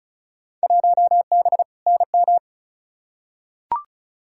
Morse code, heard as a keyed tone, about 34 wpm, 700 Hz. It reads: 1B NM
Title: Field Day Exchanges Encore - 35wpm